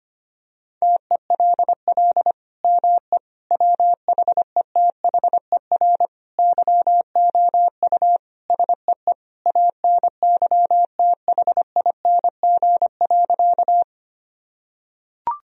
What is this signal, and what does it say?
Morse code, heard as a keyed tone, about 25 wpm, 700 Hz. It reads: TELL ME WHETHER YOU SEE ANYTHING.
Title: Sentences from Top 1000 Words - Review Part 3 - 25wpm